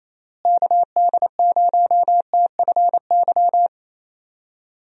Morse code, heard as a keyed tone, about 28 wpm, 700 Hz. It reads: KD0TFY
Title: USA Call Signs - 6 Characters - Repeated 2x - 28wpm